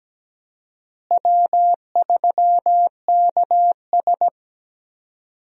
Morse code, heard as a keyed tone, about 17 wpm, 700 Hz. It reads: W3KS
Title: USA Call Signs - 4 Characters - Repeated 3x - 17wpm